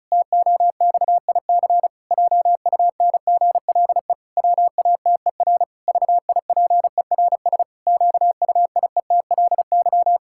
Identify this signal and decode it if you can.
Morse code, heard as a keyed tone, about 36 wpm, 700 Hz. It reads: TOXIC JUNGLE WATER VIPERS QUIETLY
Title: Pangram Challenge v2 - 35wpm